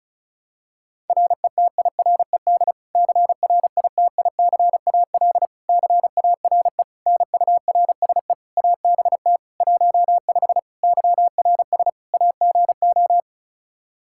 Morse code, heard as a keyed tone, about 36 wpm, 700 Hz. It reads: RETIRED CRITICAL CARE NURSE ABT 15 YRS AGO
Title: Practice QSO - Exchange 3 - More Info - 35wpm